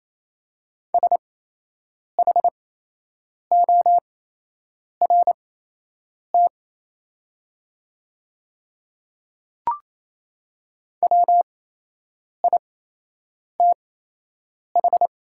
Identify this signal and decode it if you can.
Morse code, heard as a keyed tone, about 28 wpm, spaced as for 8 wpm, 700 Hz. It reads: SHORT WITH
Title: 8x Character Spacing - Top 500 Words - 28wpm